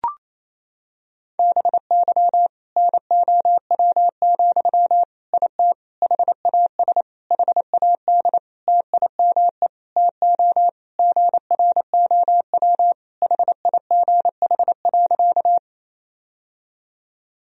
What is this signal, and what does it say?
Morse code, heard as a keyed tone, about 28 wpm, 700 Hz. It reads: BY NOW, IT HAS HAD TIME TO GROW HIGH.
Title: Sentences from Top 1000 Words - Review Part 1 - 28wpm